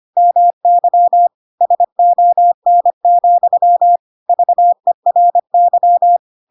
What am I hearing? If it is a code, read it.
Morse code, heard as a keyed tone, about 25 wpm, 700 Hz. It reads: MY SON, VERY
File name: Pinocchio / Pinocchio ch3 25 25 ik1hge